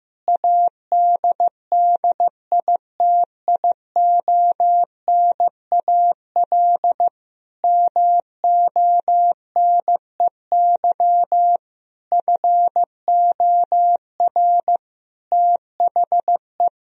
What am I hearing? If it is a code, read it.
Morse code, heard as a keyed tone, about 16 wpm, 700 Hz. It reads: ADDITIONAL MONEY FOR THE
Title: Sentences from Top 1000 Words - Review Part 4 - 15wpm